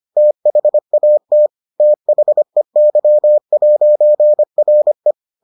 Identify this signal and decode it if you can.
Morse code, heard as a keyed tone, about 25 wpm, 600 Hz. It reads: THAT THEY'RE